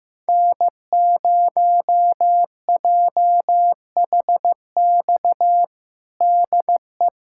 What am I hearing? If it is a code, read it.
Morse code, heard as a keyed tone, about 16 wpm, 700 Hz. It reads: N0JHX DE
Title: USA Callsign DE Callsign - 5 Characters - 15wpm